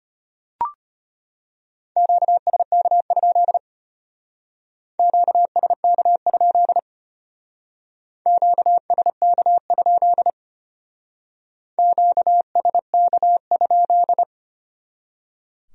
Morse code, heard as a keyed tone, about 32 wpm, 700 Hz. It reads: QSK? QSK? QSK? QSK?